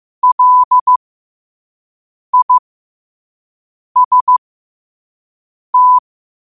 Morse code, heard as a keyed tone, about 15 wpm, spaced as for 5 wpm, 1000 Hz. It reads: LIST
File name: qotdfiles / 5wpm